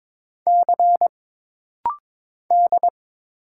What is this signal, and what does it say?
Morse code, heard as a keyed tone, about 22 wpm, 700 Hz. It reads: C D
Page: This rapid-fire timing is geared to push your ICR (Instant Character Recognition) skills.